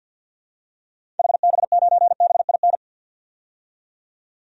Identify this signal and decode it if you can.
Morse code, heard as a keyed tone, about 51 wpm, 700 Hz. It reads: HB9BIN
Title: DXCC-100 Call Signs - 6 Characters - 50wpm